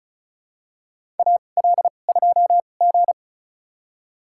Morse code, heard as a keyed tone, about 36 wpm, spaced as for 25 wpm, 700 Hz. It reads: AL2G
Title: USA Call Signs - 4 Characters - 2x Character Spacing - 35wpm